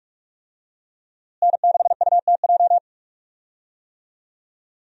Morse code, heard as a keyed tone, about 47 wpm, 700 Hz. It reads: N6UTJ
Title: USA Call Signs - 5 Characters - Repeated 2x - 45wpm